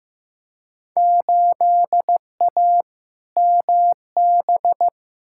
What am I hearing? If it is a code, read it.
Morse code, heard as a keyed tone, about 15 wpm, 700 Hz. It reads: 8A MB
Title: Field Day Exchanges Encore - 15wpm